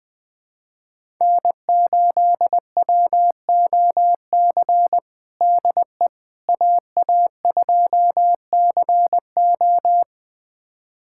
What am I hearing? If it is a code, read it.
Morse code, heard as a keyed tone, about 21 wpm, 700 Hz. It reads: N8WOC DE AA2CO
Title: USA Callsign DE Callsign - 5 Characters - 20wpm